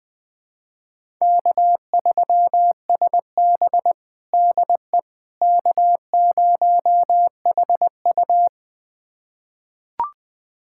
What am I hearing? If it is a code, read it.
Morse code, heard as a keyed tone, about 21 wpm, 700 Hz. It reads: K3SB DE K0HU
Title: USA Callsign DE Callsign - 4 Characters - 20wpm